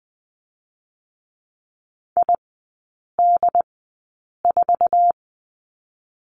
Morse code, heard as a keyed tone, about 20 wpm, spaced as for 8 wpm, 700 Hz. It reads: ID4
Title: 087 Lesson 15e - Number 5 - Three Characters 20wpm